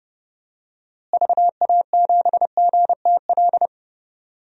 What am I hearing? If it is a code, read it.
Morse code, heard as a keyed tone, about 32 wpm, 700 Hz. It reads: VA7GTL
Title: DXCC-100 Call Signs - 6 Characters - Repeated 3x - 30wpm